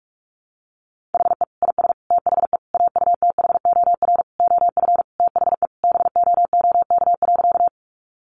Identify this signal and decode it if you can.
Morse code, heard as a keyed tone, about 44 wpm, 700 Hz. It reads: HE IS THE AUTHOR OF THE BOOK.
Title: Sentences from Top 1200 Words 45wpm